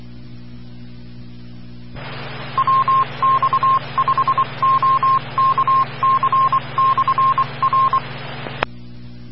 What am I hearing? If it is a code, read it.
Morse code, heard as a keyed tone, about 24 wpm, 1060 Hz, with noise. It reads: WX5OKC/R
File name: RR Skywarn / WX5OKC Amateur Repeater Hz